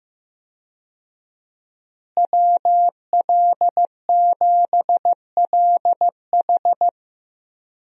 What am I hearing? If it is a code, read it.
Morse code, heard as a keyed tone, about 15 wpm, 700 Hz. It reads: WL7LH
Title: USA Call Signs - 5 Characters - Repeated 2x - 15wpm